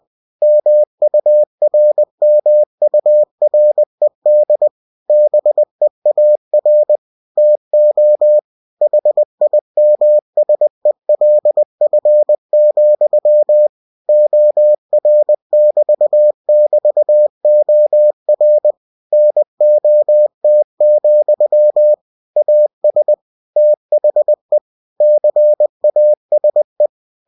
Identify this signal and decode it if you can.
Morse code, heard as a keyed tone, about 20 wpm, 600 Hz. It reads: MURMURED BEAR TO HIMSELF, OR<BT><BT>OR NOT, AS THE CASE
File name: winnie-20wpm